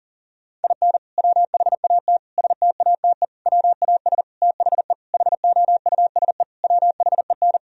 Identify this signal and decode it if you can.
Morse code, heard as a keyed tone, about 40 wpm, 700 Hz. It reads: IN WHAT STATE WAS THE HOUSE WHEN
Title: Sentences from Top 1000 Words - Review Part 2 - 40wpm